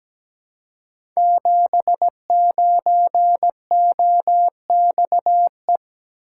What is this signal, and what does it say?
Morse code, heard as a keyed tone, about 17 wpm, 700 Hz. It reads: 79OXE
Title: Code Groups - 5 Letter:Numbers - 17wpm